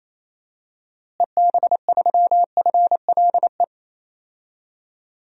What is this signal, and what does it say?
Morse code, heard as a keyed tone, about 28 wpm, 700 Hz. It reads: EB3FLE
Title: DXCC-100 Call Signs - 6 Characters - Repeated 2x - 28wpm